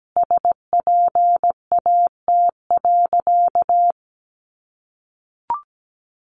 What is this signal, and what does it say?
Morse code, heard as a keyed tone, about 17 wpm, 700 Hz. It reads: SPAT.
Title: A Sound Of Thunder 17wpm